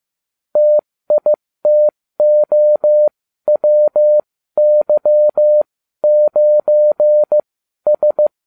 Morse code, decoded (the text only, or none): TITOWY9S